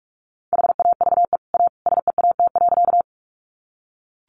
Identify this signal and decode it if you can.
Morse code, heard as a keyed tone, about 44 wpm, 700 Hz. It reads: HAVE A SEAT.
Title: Sentences from Top 1500 Words 45wpm